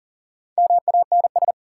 Morse code, heard as a keyed tone, about 39 wpm, 700 Hz. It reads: MANS
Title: Sentences from Top 1000 Words - Review Part 6 - New Word Drill - 40wpm